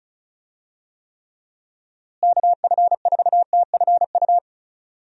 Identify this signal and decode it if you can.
Morse code, heard as a keyed tone, about 36 wpm, 700 Hz. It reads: KF4TFU
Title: USA Call Signs - 6 Characters - Repeated 3x - 35wpm